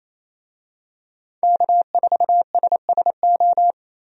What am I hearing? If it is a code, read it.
Morse code, heard as a keyed tone, about 28 wpm, 700 Hz. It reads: K4SSO
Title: USA Call Signs - 5 Characters - Repeated 2x - 28wpm